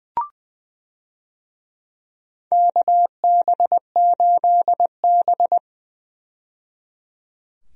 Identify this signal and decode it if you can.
Morse code, heard as a keyed tone, about 20 wpm, 700 Hz. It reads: KB8B
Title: USA Call Signs - 4 Characters - 20wpm